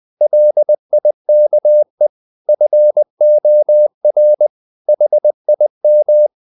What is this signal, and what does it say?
Morse code, heard as a keyed tone, about 20 wpm, 600 Hz. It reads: LIKE FOR HIM
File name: the old man and the sea / the old man and the sea part 1 20wpm